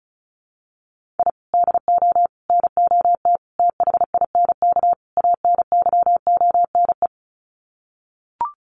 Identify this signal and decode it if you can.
Morse code, heard as a keyed tone, about 34 wpm, 700 Hz. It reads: I DO NOT THINK ANYONE
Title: Sets of 5 Words Encore 35wpm